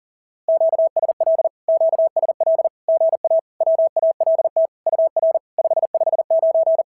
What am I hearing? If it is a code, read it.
Morse code, heard as a keyed tone, about 41 wpm, 650 Hz. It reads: QSL QSL GA WALT UR 559
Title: SOTA QSO Practice 40wpm